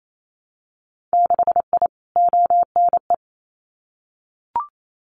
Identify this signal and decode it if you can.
Morse code, heard as a keyed tone, about 28 wpm, 700 Hz. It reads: 6I ONE
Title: Winter Field Day Exchanges Encore 28wpm